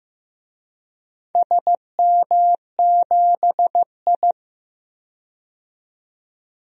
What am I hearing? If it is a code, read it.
Morse code, heard as a keyed tone, about 15 wpm, 700 Hz. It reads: SM7I
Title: DXCC-100 Call Signs - 4 Characters - 15wpm